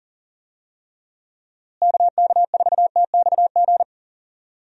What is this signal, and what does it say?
Morse code, heard as a keyed tone, about 40 wpm, 700 Hz. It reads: KK4TXG
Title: USA Call Signs - 6 Characters - Repeated 2x - 40wpm